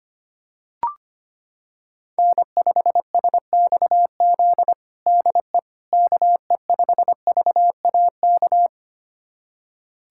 Morse code, heard as a keyed tone, about 25 wpm, 700 Hz. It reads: N5SXZ DE KE5VAK
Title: USA Callsign DE Callsign 25wpm